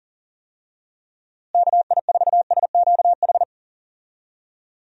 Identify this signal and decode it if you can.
Morse code, heard as a keyed tone, about 37 wpm, 700 Hz. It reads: KI4SQH
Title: USA Call Signs - 6 Characters - Repeated 3x - 40wpm